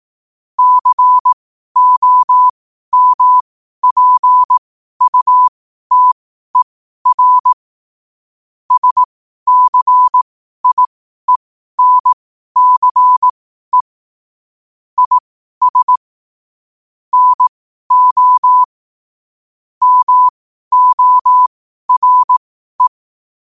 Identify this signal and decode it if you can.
Morse code, heard as a keyed tone, about 18 wpm, spaced as for 12 wpm, 1000 Hz. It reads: COMPUTER SCIENCE IS NO MORE
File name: qotdfiles / 12wpm